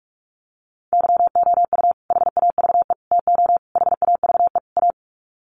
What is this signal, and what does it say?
Morse code, heard as a keyed tone, about 44 wpm, 700 Hz. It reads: YOU HAVE TO HAVE A
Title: Sets of 5 Words Encore 45wpm